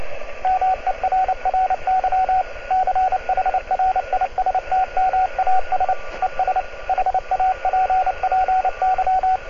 Morse code, heard as a keyed tone, about 29 wpm, 750 Hz, with radio noise. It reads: MERRY CHRISTMAS ES HAPPY